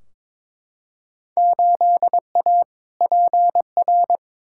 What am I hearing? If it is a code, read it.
Morse code, heard as a keyed tone, about 22 wpm, 700 Hz. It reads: 8A PR
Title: Field Day Exchanges Encore - 22wpm